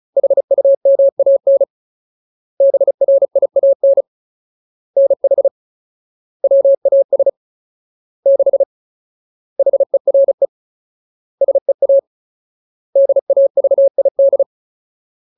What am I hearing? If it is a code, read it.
Morse code, heard as a keyed tone, about 34 wpm, 550 Hz. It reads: HUMAN BRIAN NH WAS 6 HERE SEA DAVID
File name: Ultimate Word List / 35WPM Ultimate